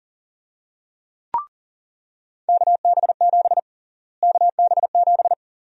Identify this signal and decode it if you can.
Morse code, heard as a keyed tone, about 40 wpm, 700 Hz. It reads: KB7 KB7
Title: USA Call Sign Prefixes - Repeated 2x - 40wpm